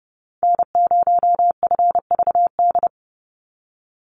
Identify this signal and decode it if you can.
Morse code, heard as a keyed tone, about 30 wpm, 700 Hz. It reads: N0FVD
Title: USA Call Signs 30wpm